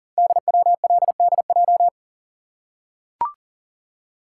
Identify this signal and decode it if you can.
Morse code, heard as a keyed tone, about 40 wpm, 690 Hz. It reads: DWLDJ
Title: Code Groups - 5 Letters - 40wpm